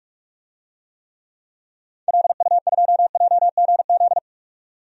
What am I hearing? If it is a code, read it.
Morse code, heard as a keyed tone, about 46 wpm, 700 Hz. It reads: LU2JGZ
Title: DXCC-100 Call Signs - 6 Characters - 45wpm